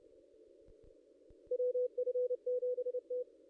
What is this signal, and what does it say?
Morse code, heard as a keyed tone, about 28 wpm, 500 Hz. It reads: WF7T
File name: contests / ss-cw-k5tr-op-n5rz -1700